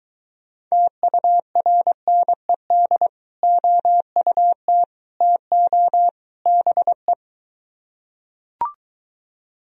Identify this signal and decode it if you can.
Morse code, heard as a keyed tone, about 23 wpm, 700 Hz. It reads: TURNED OUT TO BE